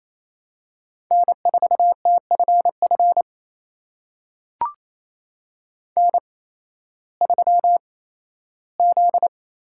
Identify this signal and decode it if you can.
Morse code, heard as a keyed tone, about 27 wpm, spaced as for 15 wpm, 700 Hz. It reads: N4TFF N3Z
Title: USA Call Signs - 5 Characters - 8x Character Spacing - 28wpm